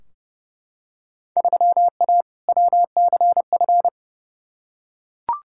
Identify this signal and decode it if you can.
Morse code, heard as a keyed tone, about 30 wpm, 700 Hz. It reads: 3A WCF
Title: Field Day Exchanges Encore - 30wpm